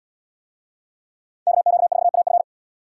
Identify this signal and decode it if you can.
Morse code, heard as a keyed tone, about 77 wpm, 700 Hz. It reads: KQ4AF